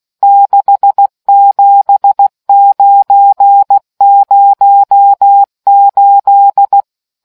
Morse code, heard as a keyed tone, about 16 wpm, 800 Hz. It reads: 67908